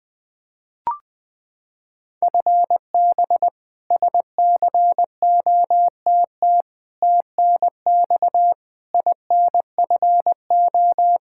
Morse code, heard as a keyed tone, about 21 wpm, 700 Hz. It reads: FB SCOTT TNX INFO
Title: K1USN SST QSO Practice 20wpm